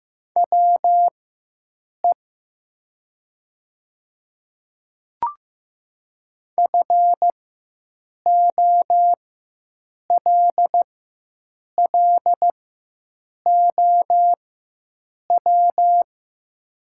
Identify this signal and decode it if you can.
Morse code, heard as a keyed tone, about 15 wpm, spaced as for 7 wpm, 700 Hz. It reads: WE FOLLOW
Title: 4x Character Spacing - Top 200 Words - 15wpm